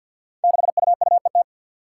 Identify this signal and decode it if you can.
Morse code, heard as a keyed tone, about 53 wpm, 700 Hz. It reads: 6UUET
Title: Five Challenging Letter-Number Combinations 50wpm